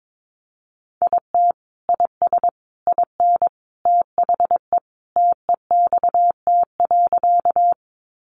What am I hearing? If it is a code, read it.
Morse code, heard as a keyed tone, about 22 wpm, 700 Hz. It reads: IT IS IN THE TEXT.
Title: Sentences from Top 1200 Words 22wpm